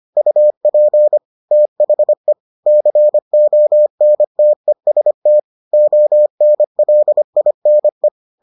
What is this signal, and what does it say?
Morse code, heard as a keyed tone, about 25 wpm, 600 Hz. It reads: UP THE CONTEST ONLINE